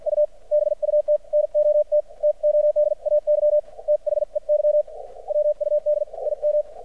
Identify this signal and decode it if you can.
Morse code, heard as a keyed tone, about 53 wpm, spaced as for 45 wpm, 600 Hz, with radio noise. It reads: U BUT NOT TODAY THEY WUD IM
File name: archived / CRSnet